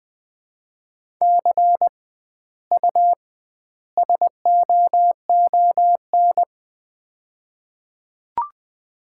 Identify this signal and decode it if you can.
Morse code, heard as a keyed tone, about 20 wpm, 700 Hz. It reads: C U SOON
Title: Common QSO Phrases - 2x Word-Spacing - 20wpm